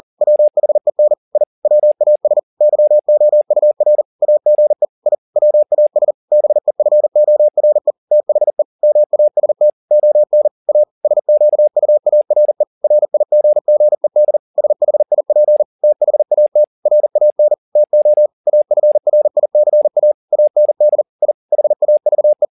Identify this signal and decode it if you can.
Morse code, heard as a keyed tone, about 40 wpm, 600 Hz. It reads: WHEN I WAS YOUR AGE I WAS BEFORE THE MAST ON A SQUARE RIGGED SHIP THAT RAN TO AFRICA AND I HAVE